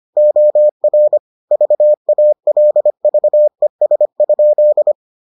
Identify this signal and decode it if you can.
Morse code, heard as a keyed tone, about 25 wpm, 600 Hz. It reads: OR VALVES?